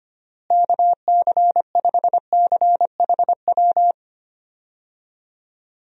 Morse code, heard as a keyed tone, about 25 wpm, 700 Hz. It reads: KC5CHW
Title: USA Call Signs - 6 Characters - Repeated 3x - 25wpm